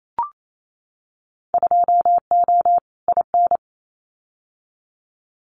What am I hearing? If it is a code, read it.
Morse code, heard as a keyed tone, about 28 wpm, 700 Hz. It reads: 2O IN
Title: Winter Field Day Exchanges 28wpm